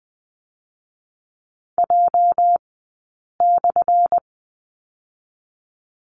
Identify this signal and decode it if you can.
Morse code, heard as a keyed tone, about 20 wpm, 700 Hz. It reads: J /
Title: 204 Lesson 33d - Letter j - Two Characters 20wpm